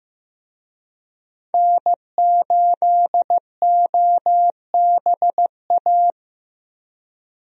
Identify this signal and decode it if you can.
Morse code, heard as a keyed tone, about 15 wpm, 700 Hz. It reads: N8OBA
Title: USA Call Signs - 5 Characters - Repeated 3x - 15wpm